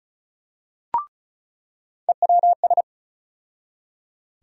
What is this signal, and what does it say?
Morse code, heard as a keyed tone, about 36 wpm, 700 Hz. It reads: EWS
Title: USA Call Sign Postfixes - 3 Letters 35wpm